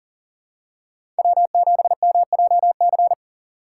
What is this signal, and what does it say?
Morse code, heard as a keyed tone, about 37 wpm, 700 Hz. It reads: W7MJC
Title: USA Call Signs - 2x Character Spacing - 40wpm